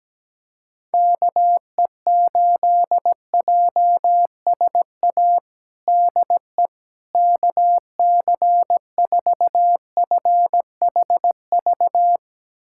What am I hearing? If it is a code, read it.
Morse code, heard as a keyed tone, about 17 wpm, 700 Hz. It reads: KE8JSA DE KC4FHV
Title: USA Callsign DE Callsign - 6 Characters - 17wpm